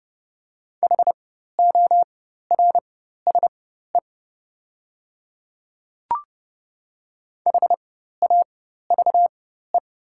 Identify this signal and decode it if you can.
Morse code, heard as a keyed tone, about 30 wpm, spaced as for 14 wpm, 700 Hz. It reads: HORSE HAVE
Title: 4x Character Spacing - Top 500 Words - 30wpm